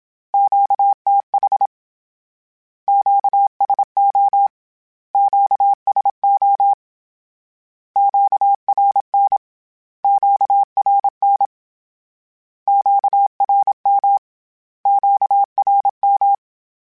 Morse code, heard as a keyed tone, about 27 wpm, 800 Hz. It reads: QTH QSO QSO QRN QRN QRM QRM
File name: Q Calls / Q Calls-mp3s / Q Calls Twice @25